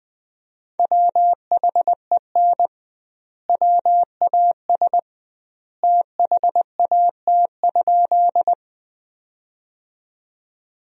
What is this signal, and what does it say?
Morse code, heard as a keyed tone, about 21 wpm, 700 Hz. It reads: WHEN WAS THAT?
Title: 2x Word-Spacing - Sentences from Top 100 Words - 20wpm